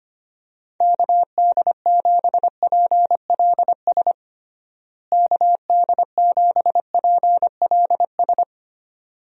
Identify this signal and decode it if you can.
Morse code, heard as a keyed tone, about 25 wpm, 700 Hz. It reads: KD7PLS KD7PLS
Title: USA Call Signs - 6 Characters - Repeated 2x - 25wpm